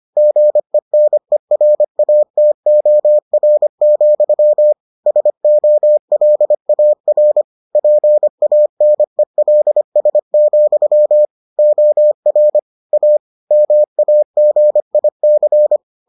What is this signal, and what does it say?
Morse code, heard as a keyed tone, about 25 wpm, 600 Hz. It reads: GENERATOR, SOLAR PANELS, OR A MAGIC